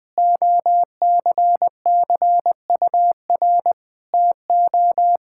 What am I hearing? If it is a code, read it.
Morse code, heard as a keyed tone, about 20 wpm, 700 Hz. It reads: OCCUR TO
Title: Sentences from Top 1000 Words - Review Part 6 - 20wpm